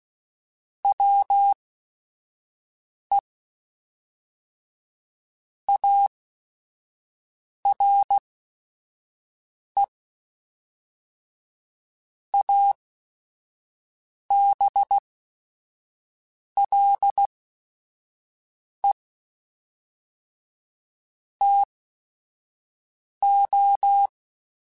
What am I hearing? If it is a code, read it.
Morse code, heard as a keyed tone, about 16 wpm, spaced as for 5 wpm, 800 Hz. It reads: WE ARE ABLE TO
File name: QSOs / QSOs-mp3s / WA7QOX 1@5